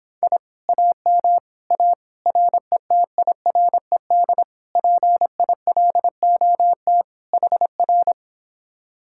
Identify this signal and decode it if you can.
Morse code, heard as a keyed tone, about 26 wpm, 700 Hz. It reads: I AM A RETIRED PILOT HR